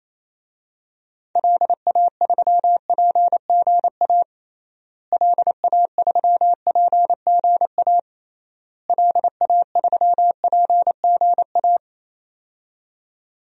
Morse code, heard as a keyed tone, about 28 wpm, 700 Hz. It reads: LA3PGA LA3PGA LA3PGA
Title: DXCC-100 Call Signs - 6 Characters - Repeated 3x - 28wpm